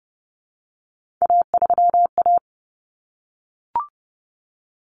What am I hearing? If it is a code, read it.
Morse code, heard as a keyed tone, about 30 wpm, 700 Hz. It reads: A3A
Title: Common Cut Numbers - 3-Digit 30wpm